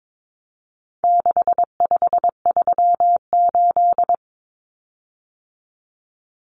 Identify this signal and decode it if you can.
Morse code, heard as a keyed tone, about 22 wpm, 700 Hz. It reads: 6538
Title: Common Cut Numbers - 4-Digit 22wpm